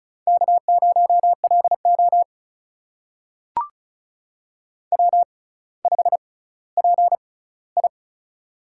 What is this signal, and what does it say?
Morse code, heard as a keyed tone, about 36 wpm, spaced as for 18 wpm, 700 Hz. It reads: K0LO W5PI
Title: USA Call Signs - 4 Characters - 6x Character Spacing - 35wpm